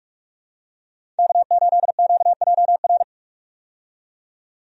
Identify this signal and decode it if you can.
Morse code, heard as a keyed tone, about 43 wpm, 700 Hz. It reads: K8QJR
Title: USA Call Signs - 5 Characters - 45wpm